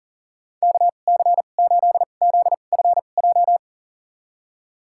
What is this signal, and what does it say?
Morse code, heard as a keyed tone, about 39 wpm, spaced as for 29 wpm, 690 Hz. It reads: KC8ZFJ
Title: USA Call Signs - 6 Characters - 2x Character Spacing - 40wpm